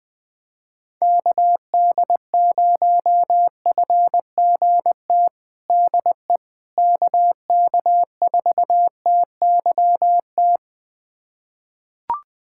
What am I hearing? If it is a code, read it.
Morse code, heard as a keyed tone, about 20 wpm, 700 Hz. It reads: KD0FGT DE KK4TYT
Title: USA Callsign DE Callsign - 6 Characters - 20wpm